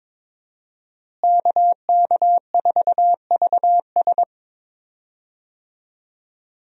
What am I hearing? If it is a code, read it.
Morse code, heard as a keyed tone, about 22 wpm, 700 Hz. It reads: KK4VS
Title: USA Call Signs - 5 Characters - Repeated 3x - 22wpm